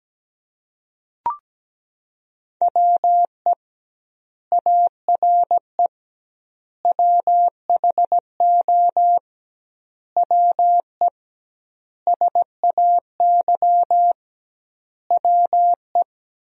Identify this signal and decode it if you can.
Morse code, heard as a keyed tone, about 17 wpm, 700 Hz. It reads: WE ARE WHO WE SAY WE
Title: 2x Word-Spacing - Sentences from Top 200 Words - 17wpm